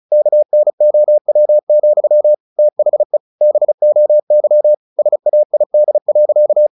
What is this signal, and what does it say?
Morse code, heard as a keyed tone, about 34 wpm, 600 Hz. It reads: KNOW, THE BOY SAID.
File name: the old man and the sea / the old man and the sea 1 35wpm